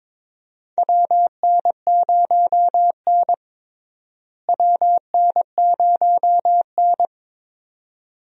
Morse code, heard as a keyed tone, about 22 wpm, 700 Hz. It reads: WN0N WN0N
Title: USA Call Signs - 4 Characters - Repeated 2x - 22wpm